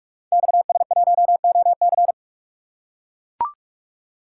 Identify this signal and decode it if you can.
Morse code, heard as a keyed tone, about 44 wpm, 700 Hz. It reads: XS1OC
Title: Code Groups - 5 Letter:Numbers - 45wpm